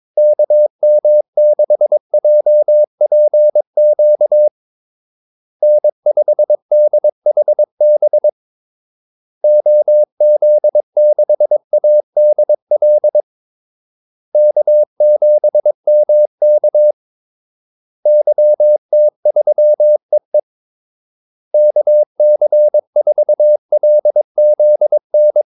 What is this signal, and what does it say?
Morse code, heard as a keyed tone, about 22 wpm, 600 Hz. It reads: KM6JPQ N5DHB OZ6ADL K7MK YT3EE KC4LZN
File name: Callsigns-22wpm